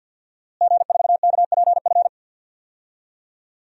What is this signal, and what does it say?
Morse code, heard as a keyed tone, about 52 wpm, 700 Hz. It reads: G4KPF
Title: DXCC-100 Call Signs - 5 Characters - Repeated 3x - 50wpm